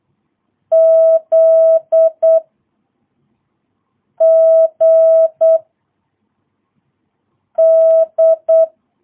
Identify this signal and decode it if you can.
Morse code, heard as a keyed tone, about 8 wpm, 650 Hz. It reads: Z G D
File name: jh8cbh / 2022cw